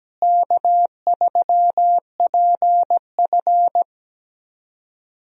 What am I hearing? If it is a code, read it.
Morse code, heard as a keyed tone, about 17 wpm, 700 Hz. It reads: K3PF
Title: USA Call Signs - 4 Characters - Repeated 2x - 17wpm